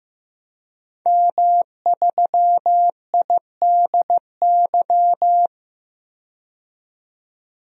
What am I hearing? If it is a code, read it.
Morse code, heard as a keyed tone, about 15 wpm, 700 Hz. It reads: M3IDY
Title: DXCC-100 Call Signs - 5 Characters - 15wpm